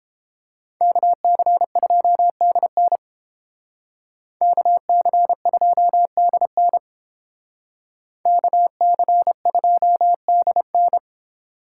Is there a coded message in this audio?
Yes: Morse code from 0.8 to 11.0 s, heard as a keyed tone, about 31 wpm, 700 Hz. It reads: KC2DN KC2DN KC2DN